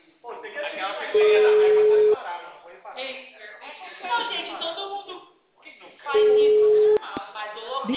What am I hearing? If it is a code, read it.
Ringback tone (Europe, 425 Hz), with speech.
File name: out- -2005- - - .25686